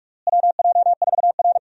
Morse code, heard as a keyed tone, about 42 wpm, 700 Hz. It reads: WJ4R
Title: USA Call Signs - 4 Characters - Repeated 3x - 45wpm